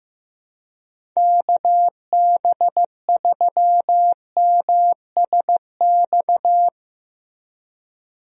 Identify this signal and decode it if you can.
Morse code, heard as a keyed tone, about 15 wpm, 700 Hz. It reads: KB3MSX
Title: USA Call Signs - 6 Characters - Repeated 3x - 15wpm